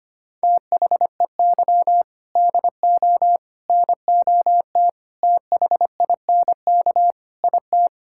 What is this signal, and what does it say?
Morse code, heard as a keyed tone, about 25 wpm, 700 Hz. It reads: THEY DO NOT THINK IT
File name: practice / 25wpm / Sentences with 4-Word Phrases - 25wpm